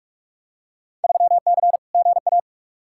Morse code, heard as a keyed tone, about 42 wpm, 700 Hz. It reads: 3C GA
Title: Field Day Exchanges Encore - 45wpm